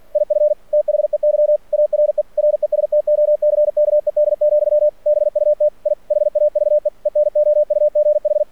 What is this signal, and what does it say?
Morse code, heard as a keyed tone, about 48 wpm, 600 Hz. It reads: IF THEY ARE RESTOCKED, BUT I HAVE ENOUGH
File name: archived / CRSnet